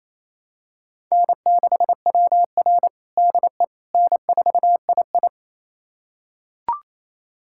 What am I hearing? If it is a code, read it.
Morse code, heard as a keyed tone, about 28 wpm, 700 Hz. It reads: N6WR DE N4II
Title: USA Callsign DE Callsign - 4 Characters - 28wpm